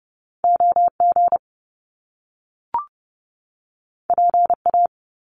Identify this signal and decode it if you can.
Morse code, heard as a keyed tone, about 30 wpm, 700 Hz. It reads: O G P A
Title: Top Two Letter Combinations in Words 30wpm